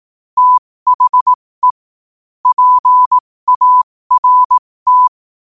Morse code, heard as a keyed tone, about 18 wpm, spaced as for 15 wpm, 1000 Hz. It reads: THE PART
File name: qotdfiles / 15wpm